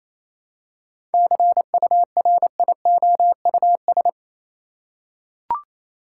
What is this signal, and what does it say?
Morse code, heard as a keyed tone, about 28 wpm, 700 Hz. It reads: CURIOUS
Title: Seven Letter Words - 28wpm